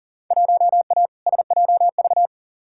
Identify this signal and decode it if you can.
Morse code, heard as a keyed tone, about 40 wpm, 700 Hz. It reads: 1A SJV
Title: Field Day Exchanges Encore - 40wpm